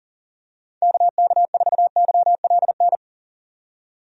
Morse code, heard as a keyed tone, about 40 wpm, 700 Hz. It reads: KK4YLN
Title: USA Call Signs - 6 Characters - 40wpm